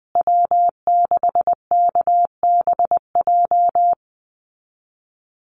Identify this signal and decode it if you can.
Morse code, heard as a keyed tone, about 20 wpm, 700 Hz. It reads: W6KBJ
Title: USA Call Signs Repeated 3x 20wpm